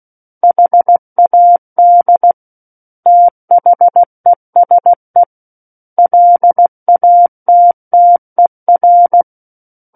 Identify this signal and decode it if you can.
Morse code, heard as a keyed tone, about 16 wpm, 700 Hz. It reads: HAD THESE LATTER
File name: Book One / 07-Chapter 7 - 16 WPM